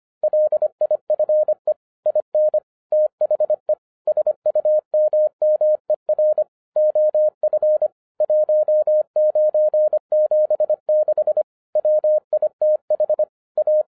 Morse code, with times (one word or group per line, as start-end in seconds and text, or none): 0.2-1.7 LIFE
2.1-2.6 IN
2.9-3.7 THE
4.1-6.4 SUMMER
6.8-7.9 OF
8.2-11.4 1976
11.8-13.2 WITH
13.6-13.8 A